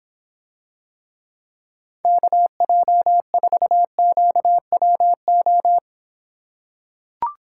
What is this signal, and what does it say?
Morse code, heard as a keyed tone, about 26 wpm, 700 Hz. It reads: KJ4QWO